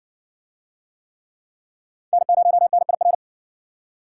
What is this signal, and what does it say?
Morse code, heard as a keyed tone, about 56 wpm, 700 Hz. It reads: N0NIR